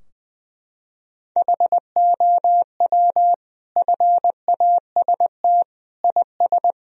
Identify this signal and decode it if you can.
Morse code, heard as a keyed tone, about 21 wpm, 700 Hz. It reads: HOW FAST IS
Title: Sentences from Top 1000 Words - Review Part 4 - 20wpm